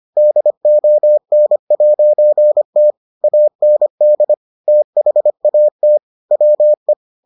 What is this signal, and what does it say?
Morse code, heard as a keyed tone, about 25 wpm, 600 Hz. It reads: DON'T AND THAT WE